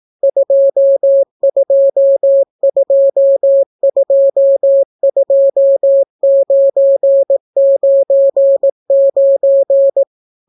Morse code, heard as a keyed tone, about 18 wpm, 550 Hz. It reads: 22222999